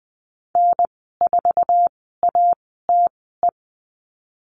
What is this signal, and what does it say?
Morse code, heard as a keyed tone, about 20 wpm, spaced as for 15 wpm, 700 Hz. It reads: N4ATE
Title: 062 Lesson 11g - Letter h - Call Signs 20wpm